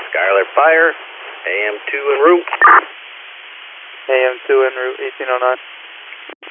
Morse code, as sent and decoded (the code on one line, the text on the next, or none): none